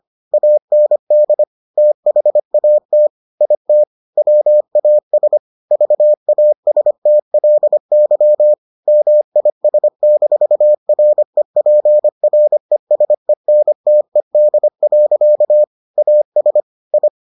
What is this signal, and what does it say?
Morse code, heard as a keyed tone, about 25 wpm, 600 Hz. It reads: AND THAT IT WAS VASTLY MIS-REPRESENTED. AS I